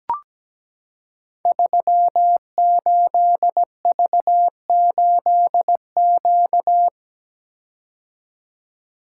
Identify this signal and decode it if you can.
Morse code, heard as a keyed tone, about 17 wpm, 700 Hz. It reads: 38V8Q
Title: Code Groups - 5 Letter:Numbers - 17wpm